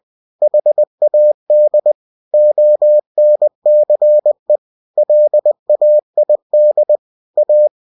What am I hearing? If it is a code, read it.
Morse code, heard as a keyed tone, about 20 wpm, 600 Hz. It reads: HAD ONCE LAID A
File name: cw-books / winnie-the-pooh / winnie-20wpm-page-0101